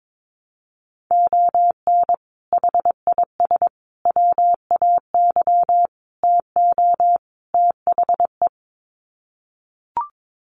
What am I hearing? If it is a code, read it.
Morse code, heard as a keyed tone, about 22 wpm, 700 Hz. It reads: ON HIS WAY TO THE
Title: Sets of 5 Words Encore2 22wpm